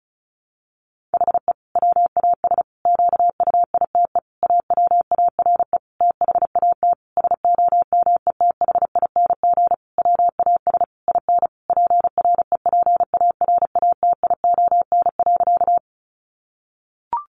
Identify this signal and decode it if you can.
Morse code, heard as a keyed tone, about 34 wpm, 700 Hz. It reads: HE WAS QUITE AWARE THAT SOMETHING WAS IN PREPARATION.
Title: Sentences from Top 1400 Words 35wpm